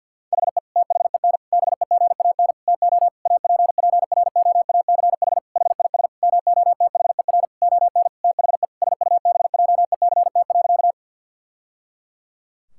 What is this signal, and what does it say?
Morse code, heard as a keyed tone, about 50 wpm, 700 Hz. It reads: HE THEN BEGAN TO APPROACH HIS MOTHER ON THE SUBJECT.
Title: Sentences from Top 1000 Words - Review Part 6 - 50wpm